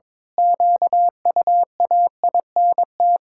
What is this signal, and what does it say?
Morse code, heard as a keyed tone, about 22 wpm, 700 Hz. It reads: QUAINT
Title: Pangram Challenge v2 - 22wpm